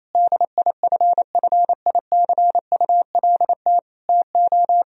Morse code, heard as a keyed tone, about 28 wpm, 700 Hz. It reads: DIFFICULT TO
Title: Sentences from Top 1000 Words - Review Part 2 - 28wpm